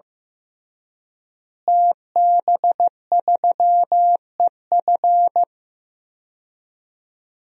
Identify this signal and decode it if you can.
Morse code, heard as a keyed tone, about 15 wpm, 700 Hz. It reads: TB3EF
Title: DXCC-100 Call Signs - 5 Characters - Repeated 3x - 15wpm